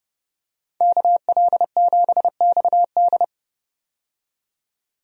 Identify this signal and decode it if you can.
Morse code, heard as a keyed tone, about 31 wpm, 690 Hz. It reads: KL7XD
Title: DXCC-100 Call Signs - 5 Characters - 30wpm